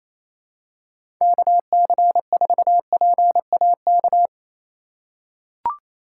Morse code, heard as a keyed tone, about 27 wpm, 700 Hz. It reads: KC4PAK
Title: USA Call Signs - 6 Characters - 3x Character Spacing - 28wpm